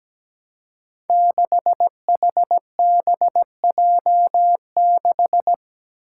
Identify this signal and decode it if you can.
Morse code, heard as a keyed tone, about 17 wpm, 700 Hz. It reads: 6HBJ6
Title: Five Challenging Letter-Number Combinations 17wpm